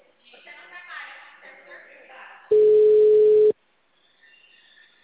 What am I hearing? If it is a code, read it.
Ringback tone (Europe, 425 Hz), with speech.